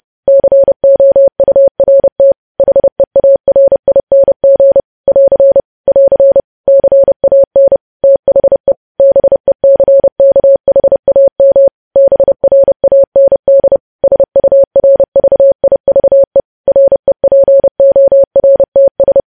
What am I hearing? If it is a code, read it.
Morse code, heard as a keyed tone, about 30 wpm, 550 Hz. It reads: COURT HEARING <AR> <AR> CAN THE BECKHAM BRAND SURVIVE REPORTS
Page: news headlines in morse code at 30 wpm